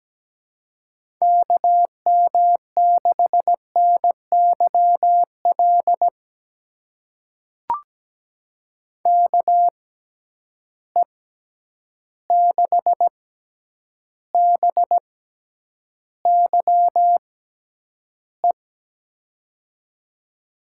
Morse code, heard as a keyed tone, about 17 wpm, spaced as for 9 wpm, 700 Hz. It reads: KM6NYL KE6BYE
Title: USA Call Signs - 6 Characters - 6x Character Spacing - 17wpm